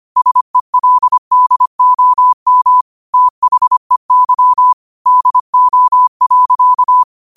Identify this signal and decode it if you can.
Morse code, heard as a keyed tone, about 25 wpm, 1000 Hz. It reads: SELDOM THEY DO.
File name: 25wpm